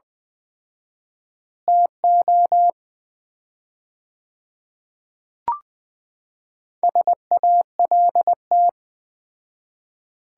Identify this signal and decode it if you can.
Morse code, heard as a keyed tone, about 20 wpm, 700 Hz. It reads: TO SALT
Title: 4x Word-Spacing - Sets of 5 Words - 20wpm